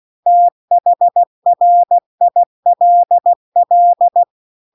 Morse code, heard as a keyed tone, about 16 wpm, 700 Hz. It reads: THRILL